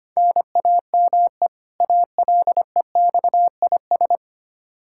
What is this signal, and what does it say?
Morse code, heard as a keyed tone, about 25 wpm, 700 Hz. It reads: NAME ALEXIS
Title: Practice QSO - Exchange 1 - Beginning - 25wpm